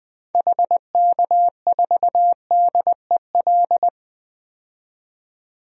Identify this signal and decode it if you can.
Morse code, heard as a keyed tone, about 21 wpm, 700 Hz. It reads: HK4DEL
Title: DXCC-100 Call Signs - 6 Characters - 20wpm